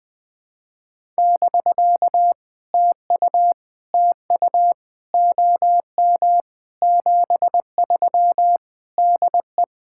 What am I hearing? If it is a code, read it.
Morse code, heard as a keyed tone, about 20 wpm, 700 Hz. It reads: <BK> TU TU OM 73 DE
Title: POTA QSO Practice - 20wpm